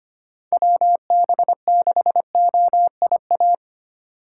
Morse code, heard as a keyed tone, about 25 wpm, 700 Hz. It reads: WB6OIA
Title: USA Call Signs - 6 Characters - 3x Character Spacing - 25wpm